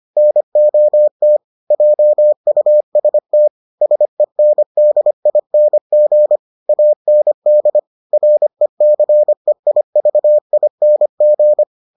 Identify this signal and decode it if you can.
Morse code, heard as a keyed tone, about 25 wpm, 600 Hz. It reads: NOT JUST SENDING AND RECEIVING